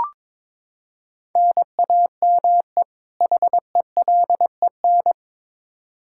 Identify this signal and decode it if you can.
Morse code, heard as a keyed tone, about 22 wpm, 700 Hz. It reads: NAME HELEN
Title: Practice QSO - Exchange 1 - Beginning - 22wpm